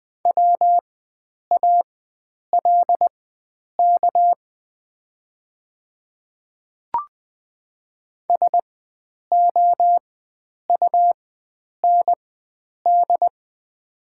Morse code, heard as a keyed tone, about 20 wpm, spaced as for 9 wpm, 700 Hz. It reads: WALK SOUND
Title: 4x Character Spacing - Top 300 Words - 20wpm